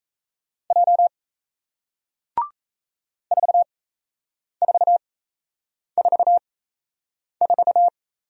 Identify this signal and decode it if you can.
Morse code, heard as a keyed tone, about 38 wpm, spaced as for 8 wpm, 700 Hz. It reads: J 4444